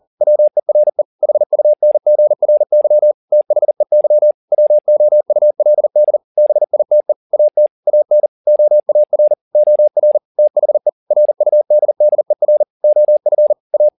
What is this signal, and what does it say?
Morse code, heard as a keyed tone, about 40 wpm, 600 Hz. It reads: WERE HUNGRY THEY WOULD BITE AT AN OAR OR THE RUDDER OF A
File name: the old man and the sea 2 40wpm